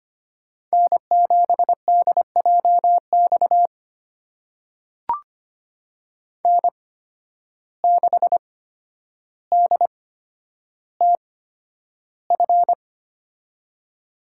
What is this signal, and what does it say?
Morse code, heard as a keyed tone, about 25 wpm, spaced as for 11 wpm, 700 Hz. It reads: N7DJX N6DTF
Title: USA Call Signs - 8x Character Spacing - 25wpm